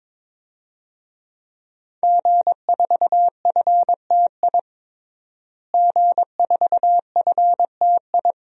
Morse code, heard as a keyed tone, about 22 wpm, 700 Hz. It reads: G4FTI G4FTI
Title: DXCC-100 Call Signs - 5 Characters - Repeated 3x - 22wpm